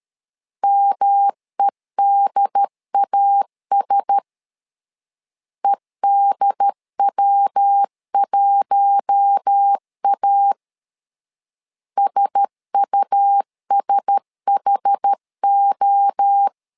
Morse code, heard as a keyed tone, about 13 wpm, 800 Hz. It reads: MEDAS EDW1A SUSHO
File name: LEZIONI / 4 Q1S 12wpm 1802